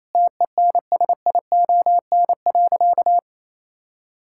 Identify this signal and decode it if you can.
Morse code, heard as a keyed tone, about 28 wpm, 700 Hz. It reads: TENSION.
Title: Sentences from Top 1600 Words 28wpm